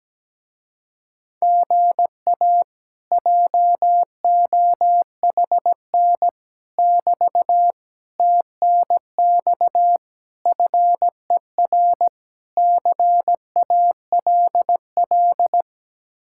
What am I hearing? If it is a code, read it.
Morse code, heard as a keyed tone, about 17 wpm, 700 Hz. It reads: GA JOHN <BT> TNX FER CALL
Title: Speed-Racing Common QSO Phrases - 17wpm